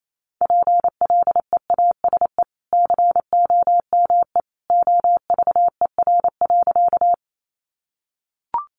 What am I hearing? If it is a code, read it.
Morse code, heard as a keyed tone, about 28 wpm, 700 Hz. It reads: PLEASE COME OVER.
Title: Sentences from Top 1500 Words 28wpm